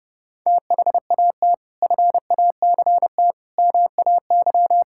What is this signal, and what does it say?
Morse code, heard as a keyed tone, about 30 wpm, 700 Hz. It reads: THAT FACT MAY
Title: Sentences from Top 1000 Words - Review Part 6 - 30wpm